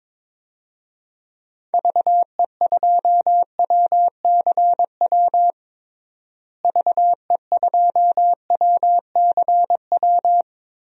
Morse code, heard as a keyed tone, about 21 wpm, 700 Hz. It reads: VE2WCW VE2WCW
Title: DXCC-100 Call Signs - 6 Characters - Repeated 3x - 22wpm